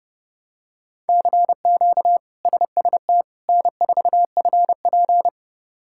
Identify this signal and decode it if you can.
Morse code, heard as a keyed tone, about 32 wpm, 700 Hz. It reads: CQ SST N4FP
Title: K1USN SST QSO Practice - 30wpm